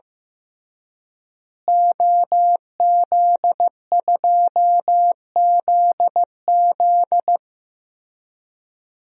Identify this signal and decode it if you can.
Morse code, heard as a keyed tone, about 15 wpm, 700 Hz. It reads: OZ2ZZ
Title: DXCC-100 Call Signs - 5 Characters - Repeated 2x - 15wpm